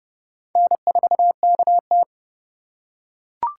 Morse code, heard as a keyed tone, about 30 wpm, 700 Hz. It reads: N4KT
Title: USA Call Signs - 4 Characters - 4x Character Spacing - 30wpm